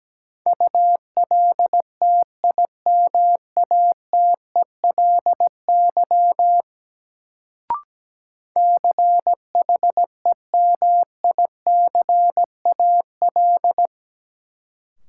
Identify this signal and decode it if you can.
Morse code, heard as a keyed tone, about 17 wpm, 700 Hz. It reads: ULTIMATELY CHEMICAL
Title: Sentences from Top 1200 Words - New Word Drill - 17wpm